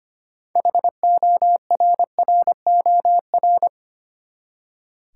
Morse code, heard as a keyed tone, about 25 wpm, 700 Hz. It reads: HORROR
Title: Six Letter Words - 25wpm